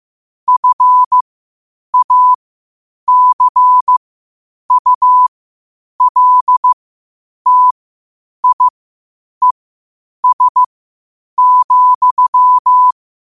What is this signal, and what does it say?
Morse code, heard as a keyed tone, about 15 wpm, spaced as for 8 wpm, 1000 Hz. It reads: FACULTIES,
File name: qotdfiles / 8wpm